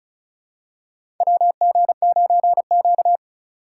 Morse code, heard as a keyed tone, about 36 wpm, 700 Hz. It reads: WG9Q
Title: USA Call Signs - 3x Character Spacing - 35wpm